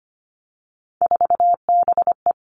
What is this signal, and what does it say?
Morse code, heard as a keyed tone, about 25 wpm, 700 Hz. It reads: 4BE
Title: Three Challenging Letter-Number Combinations 25wpm